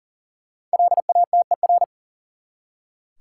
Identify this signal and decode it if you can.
Morse code, heard as a keyed tone, about 39 wpm, 700 Hz. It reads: LATER
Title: Five Letter Words - 40wpm